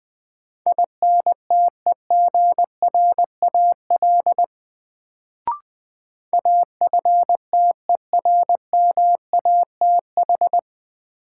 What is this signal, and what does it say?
Morse code, heard as a keyed tone, about 20 wpm, 700 Hz. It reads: INTEGRAL AFTERMATH
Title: Sets of 5 Words - New Word Drill - 20wpm